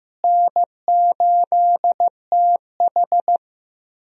Morse code, heard as a keyed tone, about 15 wpm, 700 Hz. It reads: N8TH
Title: USA Call Signs - 4 Characters - Repeated 3x - 15wpm